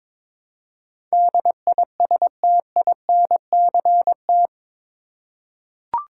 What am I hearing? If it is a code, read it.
Morse code, heard as a keyed tone, about 21 wpm, 700 Hz. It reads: DISTINCT
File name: practice / 22wpm / Eight Letter Words - 22wpm